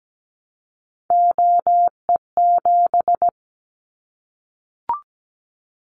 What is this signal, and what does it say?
Morse code, heard as a keyed tone, about 17 wpm, 700 Hz. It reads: OE7
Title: Callsign Prefixes for Top 100 DXCC - 17wpm